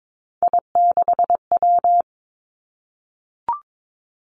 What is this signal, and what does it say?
Morse code, heard as a keyed tone, about 22 wpm, 700 Hz. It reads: I6W
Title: Three Challenging Letter-Number Combinations 22wpm